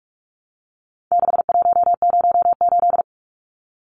Morse code, heard as a keyed tone, about 44 wpm, 700 Hz. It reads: 6108
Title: Numbers - 4-Digit 45wpm